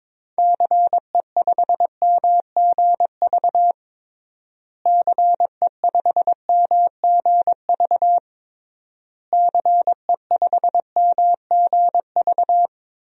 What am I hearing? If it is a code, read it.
Morse code, heard as a keyed tone, about 22 wpm, 700 Hz. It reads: CE5MGV CE5MGV CE5MGV
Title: DXCC-100 Call Signs - 6 Characters - Repeated 3x - 22wpm